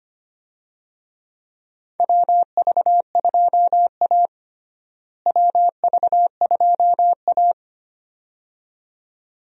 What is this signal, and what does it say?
Morse code, heard as a keyed tone, about 25 wpm, 700 Hz. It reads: WV2A WV2A
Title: USA Call Signs - 4 Characters - Repeated 2x - 25wpm